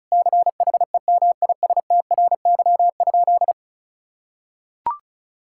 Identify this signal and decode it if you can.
Morse code, heard as a keyed tone, about 34 wpm, 700 Hz. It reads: CHEMISTRY?
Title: Sentences from Top 1000 Words - Review Part 6 - 35wpm